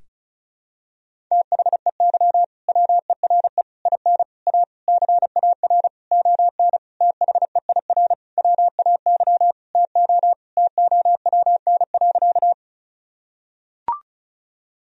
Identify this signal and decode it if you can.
Morse code, heard as a keyed tone, about 36 wpm, 700 Hz. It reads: THEY WERE IN A CAR ON THEIR WAY TO TOWN.
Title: Sentences from Top 1000 Words - Review Part 1 - 35wpm